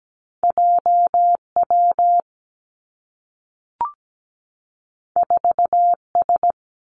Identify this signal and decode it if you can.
Morse code, heard as a keyed tone, about 17 wpm, 700 Hz. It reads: JW 4S
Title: Two Challenging Letter-Number Combinations 17wpm